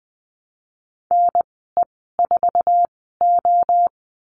Morse code, heard as a keyed tone, about 20 wpm, spaced as for 14 wpm, 700 Hz. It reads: NE4O
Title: 048 Lesson 9g - Number 4 - Call Signs 20wpm